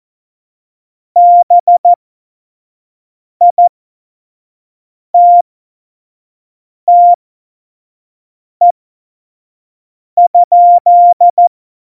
Morse code, heard as a keyed tone, about 14 wpm, spaced as for 5 wpm, 700 Hz. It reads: BITTE?
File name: Morsekurs / Witze / Witz 10. September 2019, 25 F70